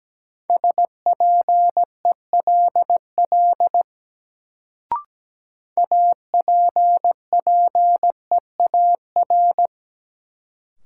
Morse code, heard as a keyed tone, about 17 wpm, 700 Hz. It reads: SPELL APPEAR
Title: Sentences from Top 1200 Words - New Word Drill Alt - 17wpm